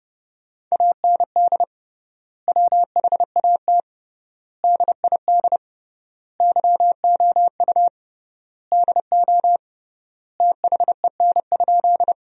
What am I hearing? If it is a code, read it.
Morse code, heard as a keyed tone, about 30 wpm, 700 Hz. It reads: AND WHAT DID YOU DO THEN?
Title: 3x Word-Spacing - Sentences from Top 100 Words - 30wpm